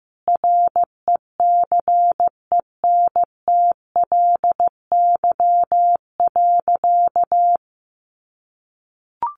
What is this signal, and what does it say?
Morse code, heard as a keyed tone, about 15 wpm, 700 Hz. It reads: RECENTLY.
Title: Sentences from Top 1200 Words 15wpm